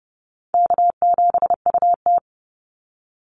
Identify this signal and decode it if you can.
Morse code, heard as a keyed tone, about 30 wpm, 700 Hz. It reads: K7UT
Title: Easy Call Signs Repeated 3x - 30wpm